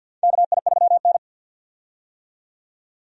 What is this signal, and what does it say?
Morse code, heard as a keyed tone, about 52 wpm, 700 Hz. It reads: KI3N
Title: USA Call Signs - 4 Characters - 50wpm